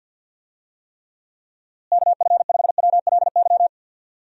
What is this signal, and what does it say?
Morse code, heard as a keyed tone, about 48 wpm, 700 Hz. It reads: KF5WLY
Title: USA Call Signs - 6 Characters - Repeated 3x - 50wpm